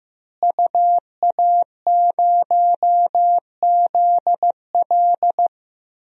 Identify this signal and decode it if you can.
Morse code, heard as a keyed tone, about 15 wpm, 700 Hz. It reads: UA0ZL
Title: DXCC-100 Call Signs - 5 Characters - Repeated 3x - 15wpm